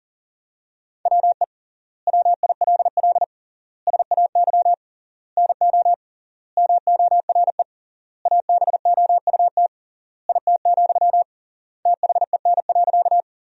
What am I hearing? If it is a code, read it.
Morse code, heard as a keyed tone, about 40 wpm, 700 Hz. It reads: WE WILL SAY NO MORE ABOUT IT, THEN.
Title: 3x Word-Spacing - Sentences from Top 200 Words - 40wpm